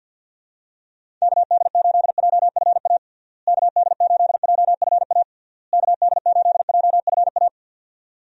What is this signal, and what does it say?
Morse code, heard as a keyed tone, about 52 wpm, 700 Hz. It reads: KD8JFA KD8JFA KD8JFA
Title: USA Call Signs - 6 Characters - Repeated 3x - 50wpm